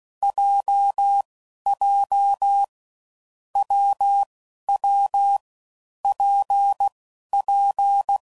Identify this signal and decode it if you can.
Morse code, heard as a keyed tone, about 16 wpm, 800 Hz. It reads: J J W W P P
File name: Combinations / Difficult Combos / WPJ WPJ1